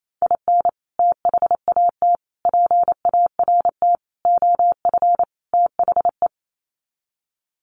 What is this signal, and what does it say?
Morse code, heard as a keyed tone, about 28 wpm, 700 Hz. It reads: IN THAT PART OF THE
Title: Sets of 5 Words Encore2 28wpm